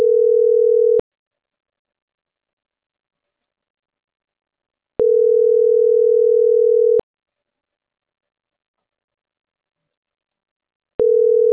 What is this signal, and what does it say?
Ringback tone (North America), 440 + 480 Hz.